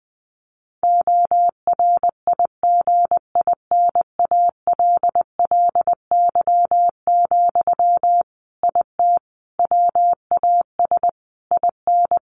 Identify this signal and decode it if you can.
Morse code, heard as a keyed tone, about 20 wpm, 700 Hz. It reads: ORIGINALLY, IT WAS IN
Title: Sentences from Top 1500 Words 20wpm